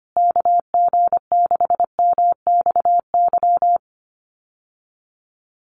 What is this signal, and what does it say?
Morse code, heard as a keyed tone, about 25 wpm, 700 Hz. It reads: KG6MXY
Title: USA Call Signs Repeated 3x 25wpm